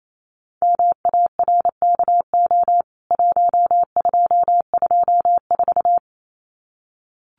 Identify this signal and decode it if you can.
Morse code, heard as a keyed tone, about 28 wpm, 700 Hz. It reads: MARKO 1224
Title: CWT Exchange 28wpm